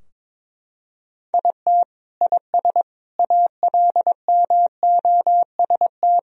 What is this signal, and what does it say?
Morse code, heard as a keyed tone, about 21 wpm, 700 Hz. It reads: IT IS ALMOST
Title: Sentences from Top 1000 Words - Review Part 4 - 22wpm